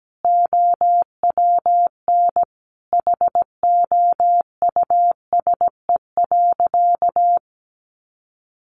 Morse code, heard as a keyed tone, about 17 wpm, 700 Hz. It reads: OWN HOUSE.
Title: Sentences from Top 300 Words 17wpm